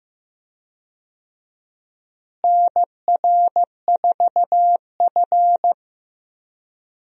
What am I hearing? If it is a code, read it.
Morse code, heard as a keyed tone, about 15 wpm, 700 Hz. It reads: NR4F
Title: USA Call Signs - 4 Characters - Repeated 2x - 15wpm